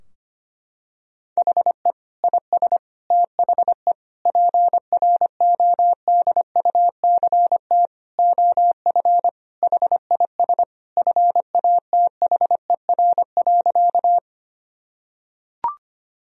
Morse code, heard as a keyed tone, about 25 wpm, 700 Hz. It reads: HE IS THE PRODUCT OF HIS FATHER.
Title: Sentences from Top 1000 Words - Review Part 2 - 25wpm